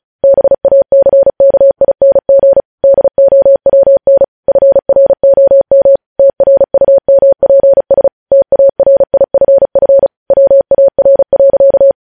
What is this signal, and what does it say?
Morse code, heard as a keyed tone, about 36 wpm, 550 Hz. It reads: BACKING DOWN FROM TRUMPS TARIFF WAR.
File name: podcast / News headlines-1x-35x35